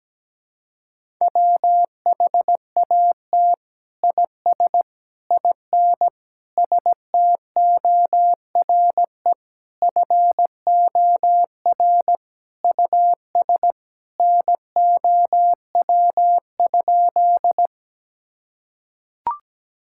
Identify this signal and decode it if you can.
Morse code, heard as a keyed tone, about 17 wpm, 700 Hz. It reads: WHAT IS IN STORE FOR US NOW?
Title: Sentences from Top 1000 Words - Review Part 5 - 17wpm